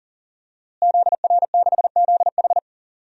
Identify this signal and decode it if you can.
Morse code, heard as a keyed tone, about 40 wpm, 700 Hz. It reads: ZR6ZH
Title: DXCC-100 Call Signs - 5 Characters - Repeated 3x - 40wpm